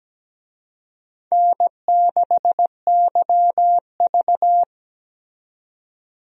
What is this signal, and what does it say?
Morse code, heard as a keyed tone, about 17 wpm, 700 Hz. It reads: N6YV
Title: USA Call Signs - 4 Characters - 17wpm